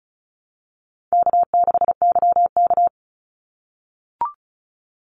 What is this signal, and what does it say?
Morse code, heard as a keyed tone, about 36 wpm, 700 Hz. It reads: K6YK
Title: NA SOTA Chaser Callsigns 35wpm